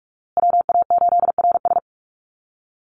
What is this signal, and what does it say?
Morse code, heard as a keyed tone, about 46 wpm, 700 Hz. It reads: WA8RS
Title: USA Call Signs Repeated 3x 45wpm